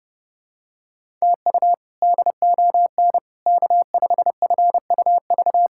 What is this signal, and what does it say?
Morse code, heard as a keyed tone, about 32 wpm, 700 Hz. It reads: TU DON K5FUV
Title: K1USN SST QSO Practice - 30wpm